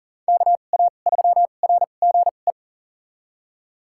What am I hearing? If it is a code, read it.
Morse code, heard as a keyed tone, about 40 wpm, spaced as for 29 wpm, 700 Hz. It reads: KA3RGE
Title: USA Call Signs - 2x Character Spacing - 40wpm